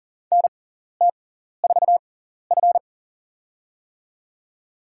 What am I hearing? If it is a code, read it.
Morse code, heard as a keyed tone, about 39 wpm, spaced as for 14 wpm, 700 Hz. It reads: NT4F
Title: USA Call Signs - 4 Characters - 6x Character Spacing - 40wpm